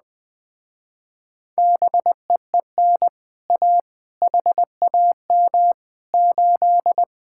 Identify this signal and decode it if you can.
Morse code, heard as a keyed tone, about 20 wpm, 700 Hz. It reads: BEEN A HAM 8
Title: Speed-Racing Common QSO Phrases Encore - 17wpm